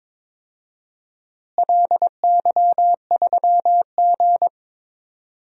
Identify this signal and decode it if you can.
Morse code, heard as a keyed tone, about 22 wpm, 700 Hz. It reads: LY3G
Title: DXCC-100 Call Signs - 4 Characters - Repeated 2x - 22wpm